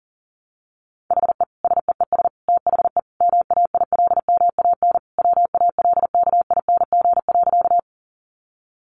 Morse code, heard as a keyed tone, about 40 wpm, 700 Hz. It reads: HE SEES THE MAILMAN WALKING.
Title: Additional Easy Phrases 40wpm